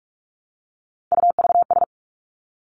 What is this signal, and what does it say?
Morse code, heard as a keyed tone, about 46 wpm, 700 Hz. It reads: UVS
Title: Three Challenging Letter-Number Combinations 45wpm